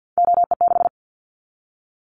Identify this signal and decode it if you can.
Morse code, heard as a keyed tone, about 50 wpm, 700 Hz. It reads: OE6
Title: Callsign Prefixes for Top 100 DXCC - 50wpm